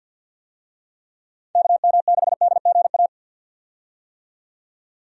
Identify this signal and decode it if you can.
Morse code, heard as a keyed tone, about 51 wpm, 700 Hz. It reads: KM6DGA
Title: USA Call Signs - 6 Characters - 50wpm